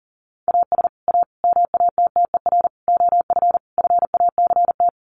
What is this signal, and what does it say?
Morse code, heard as a keyed tone, about 40 wpm, 700 Hz. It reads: AS A MATTER OF FACT
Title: Sets of 5 Words 40wpm